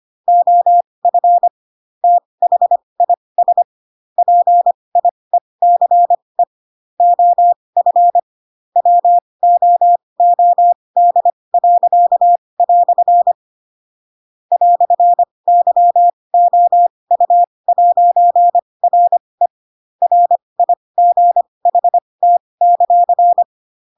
Morse code, heard as a keyed tone, about 25 wpm, spaced as for 20 wpm, 700 Hz. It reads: OF THIS PIECE OF WOOD." "YOU'RE RIGHT;
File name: Pinocchio / Pinocchio ch2 15 25 ik1hge